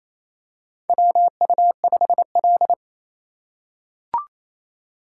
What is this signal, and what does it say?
Morse code, heard as a keyed tone, about 29 wpm, 700 Hz. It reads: WU5L
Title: USA Call Signs - 4 Characters - 2x Character Spacing - 28wpm